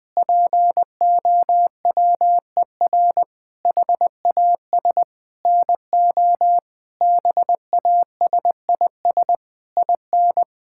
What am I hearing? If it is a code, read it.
Morse code, heard as a keyed tone, about 20 wpm, 700 Hz. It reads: POWER HAS NO BASIS IN
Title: Sentences from Top 1000 Words - Review Part 5 - 20wpm